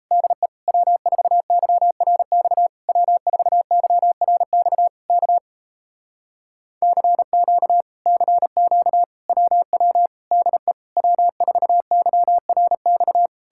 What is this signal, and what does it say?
Morse code, heard as a keyed tone, about 38 wpm, 700 Hz. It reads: DE W4YRX W4YRX K CQ CQ WW DE W4YRX